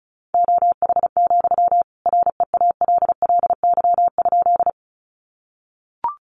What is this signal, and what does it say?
Morse code, heard as a keyed tone, about 34 wpm, 700 Hz. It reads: OH, REALLY?
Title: Sentences from Top 600 Words 35wpm